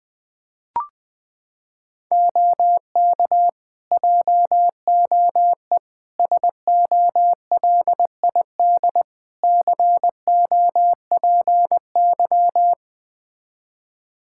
Morse code, heard as a keyed tone, about 20 wpm, 700 Hz. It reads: OK JOE SOLID COPY
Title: Practice QSO - Exchange 3 - More Info - 20wpm